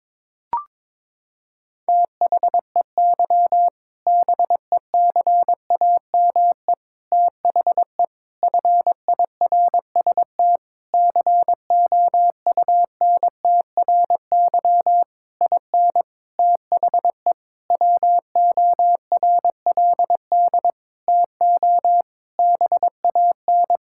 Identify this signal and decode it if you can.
Morse code, heard as a keyed tone, about 22 wpm, 700 Hz. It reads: THEY BECAME THE FIRST COUNTRY IN THE WORLD TO BAN
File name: practice / 22wpm / Sentences with 8-Word Phrases - 22wpm